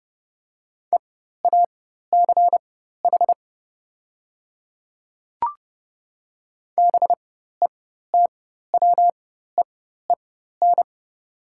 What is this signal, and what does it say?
Morse code, heard as a keyed tone, about 30 wpm, spaced as for 14 wpm, 700 Hz. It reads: EACH BETWEEN
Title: 4x Character Spacing - Top 300 Words - 30wpm